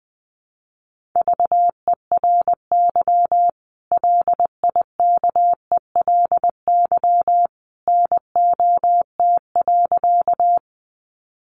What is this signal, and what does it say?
Morse code, heard as a keyed tone, about 20 wpm, 700 Hz. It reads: VERY LIKELY NOT.
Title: Sentences from Top 600 Words 20wpm